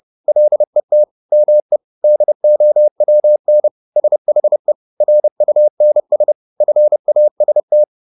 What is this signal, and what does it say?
Morse code, heard as a keyed tone, about 30 wpm, 600 Hz. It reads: LET ME DOWN SHE RUNS FAST